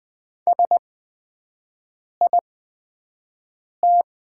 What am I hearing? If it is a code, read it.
Morse code, heard as a keyed tone, about 20 wpm, spaced as for 5 wpm, 700 Hz. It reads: SIT
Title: 8x Character Spacing - Top 500 Words - 20wpm